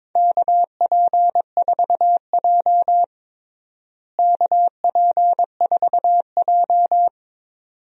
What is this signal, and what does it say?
Morse code, heard as a keyed tone, about 22 wpm, 700 Hz. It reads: KP4J KP4J
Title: DXCC-100 Call Signs - 4 Characters - Repeated 2x - 22wpm